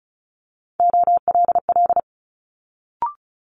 Morse code, heard as a keyed tone, about 36 wpm, 700 Hz. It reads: OLL
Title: Multi-letter Phonemes Quick Fire 35wpm